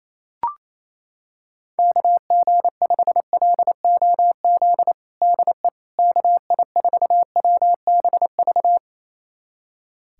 Morse code, heard as a keyed tone, about 28 wpm, 700 Hz. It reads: KG5LOZ DE KI4WBV
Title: USA Callsign DE Callsign - 6 Characters - 28wpm